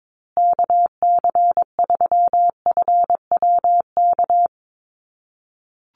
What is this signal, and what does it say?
Morse code, heard as a keyed tone, about 22 wpm, 700 Hz. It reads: KC3FWK
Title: USA Call Signs Repeated 2x 22wpm